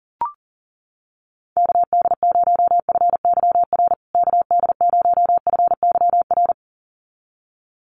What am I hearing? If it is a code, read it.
Morse code, heard as a keyed tone, about 37 wpm, 700 Hz. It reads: KD0FYR KD0FYR
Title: USA Call Signs Repeated 2x 40wpm